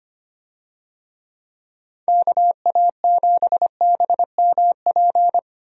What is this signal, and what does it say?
Morse code, heard as a keyed tone, about 25 wpm, 700 Hz. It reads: KA7BMP
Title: USA Call Signs - 6 Characters - 25wpm